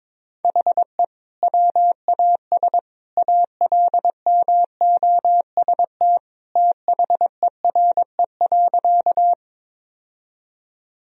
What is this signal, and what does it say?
Morse code, heard as a keyed tone, about 22 wpm, 700 Hz. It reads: HE WAS ALMOST THERE.
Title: Sentences from Top 1000 Words - Review Part 4 - 22wpm